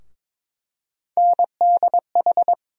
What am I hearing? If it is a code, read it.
Morse code, heard as a keyed tone, about 22 wpm, 700 Hz. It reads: NDH
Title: USA Call Sign Postfixes - 3 Letters 22wpm